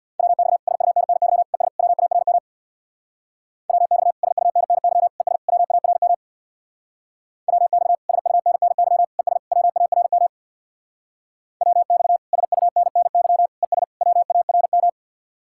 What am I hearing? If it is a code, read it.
Morse code, heard as a keyed tone, about 64 wpm, 700 Hz. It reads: WX SUNNY ES WARM WX SUNNY ES WARM WX SUNNY ES WARM WX SUNNY ES WARM